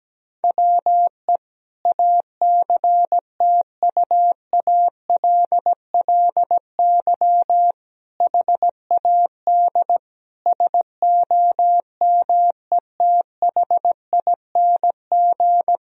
Morse code, heard as a keyed tone, about 17 wpm, 700 Hz. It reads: WE ACTUALLY HAD SOMETHING
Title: Sentences from Top 1000 Words - Review Part 4 - 17wpm